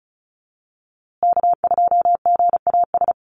Morse code, heard as a keyed tone, about 34 wpm, 700 Hz. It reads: K2GAS
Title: USA Call Signs Repeated 2x 35wpm